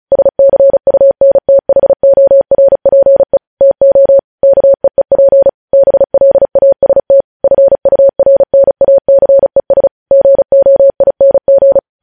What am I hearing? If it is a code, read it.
Morse code, heard as a keyed tone, about 36 wpm, 550 Hz. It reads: SCUNTHORPE TO KEEP BLAST FURNACES GOING
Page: news headlines in morse code at 35 wpm